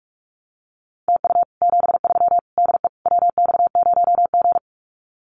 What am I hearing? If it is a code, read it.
Morse code, heard as a keyed tone, about 44 wpm, 700 Hz. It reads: TU 73 DE WX0G
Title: Common QSO Phrases Encore 45wpm